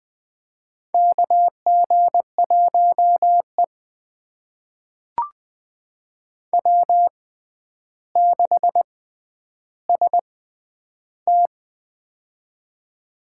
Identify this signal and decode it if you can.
Morse code, heard as a keyed tone, about 21 wpm, spaced as for 10 wpm, 700 Hz. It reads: KG1E W6ST
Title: USA Call Signs - 6x Character Spacing - 20wpm